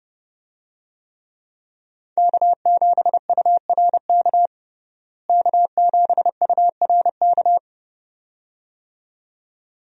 Morse code, heard as a keyed tone, about 30 wpm, 700 Hz. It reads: K7URK K7URK
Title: USA Call Signs - 5 Characters - Repeated 2x - 30wpm